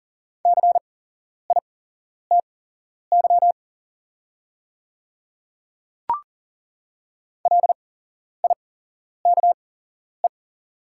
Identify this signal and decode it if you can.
Morse code, heard as a keyed tone, about 40 wpm, spaced as for 11 wpm, 700 Hz. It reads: CITY LIKE
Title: 8x Character Spacing - Top 500 Words - 40wpm